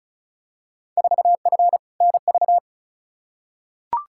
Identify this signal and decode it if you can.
Morse code, heard as a keyed tone, about 36 wpm, 700 Hz. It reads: 4F NV
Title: Field Day Exchanges Encore - 35wpm